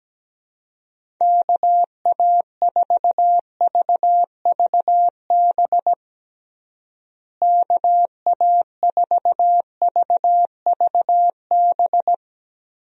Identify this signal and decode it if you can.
Morse code, heard as a keyed tone, about 17 wpm, 700 Hz. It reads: KA4VVB KA4VVB
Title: USA Call Signs - 6 Characters - Repeated 2x - 17wpm